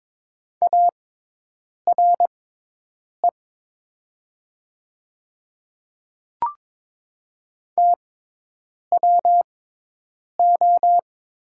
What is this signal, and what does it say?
Morse code, heard as a keyed tone, about 22 wpm, spaced as for 8 wpm, 700 Hz. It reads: ARE TWO
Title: 6x Character Spacing - Top 100 Words - 22wpm